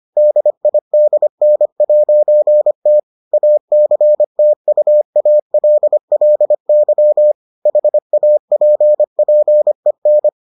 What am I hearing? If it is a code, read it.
Morse code, heard as a keyed tone, about 25 wpm, 600 Hz. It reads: DIDN'T ACTUALLY HAPPEN